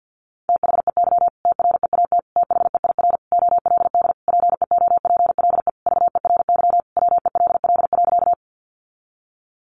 Morse code, heard as a keyed tone, about 48 wpm, 700 Hz. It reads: THEY TREAT THEIR OLD PEOPLE VERY WELL.
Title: Sentences from Top 1300 Words 50wpm